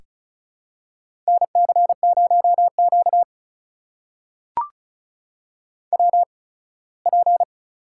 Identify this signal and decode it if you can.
Morse code, heard as a keyed tone, about 36 wpm, spaced as for 21 wpm, 700 Hz. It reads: NC0Q WP
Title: USA Call Signs - 4 Characters - 8x Character Spacing - 35wpm